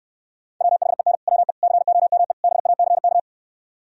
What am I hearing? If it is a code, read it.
Morse code, heard as a keyed tone, about 68 wpm, 700 Hz. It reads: WHEN WE COME BACK